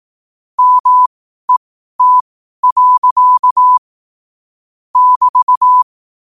Morse code, heard as a keyed tone, about 18 wpm, spaced as for 12 wpm, 1000 Hz. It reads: MET. <BT>
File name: qotdfiles / 12wpm